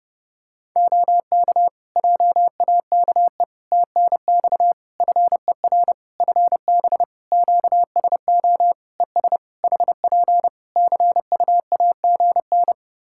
Morse code, heard as a keyed tone, about 30 wpm, 700 Hz. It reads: OK JAKE TNX FER FB QSO ES HP CUAGN
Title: Practice QSO - Exchange 4 - Ending - 30wpm